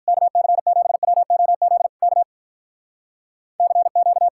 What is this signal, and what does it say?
Morse code, heard as a keyed tone, about 53 wpm, 700 Hz. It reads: KC7WOZ K CQ